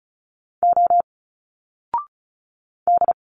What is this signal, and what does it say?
Morse code, heard as a keyed tone, about 36 wpm, 700 Hz. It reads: O D
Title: Single Letters 35wpm